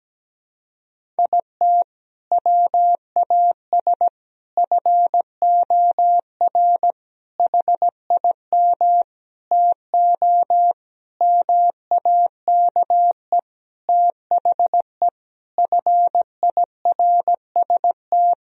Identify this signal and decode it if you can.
Morse code, heard as a keyed tone, about 17 wpm, 700 Hz. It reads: IT WAS FOR HIM TO MAKE THE FIRST
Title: Sentences from Top 1000 Words - Review Part 1 - 17wpm